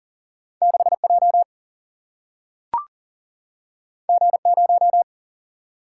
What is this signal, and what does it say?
Morse code, heard as a keyed tone, about 39 wpm, 700 Hz. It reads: 6J G0
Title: Single-Letter Single-Number 40wpm